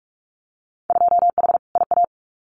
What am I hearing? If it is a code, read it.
Morse code, heard as a keyed tone, about 44 wpm, 700 Hz. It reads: 2H IA
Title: Winter Field Day Exchanges Encore 45wpm